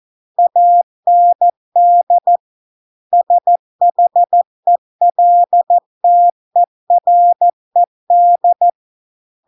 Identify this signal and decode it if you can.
Morse code, heard as a keyed tone, about 14 wpm, 700 Hz. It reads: AND SHELTERED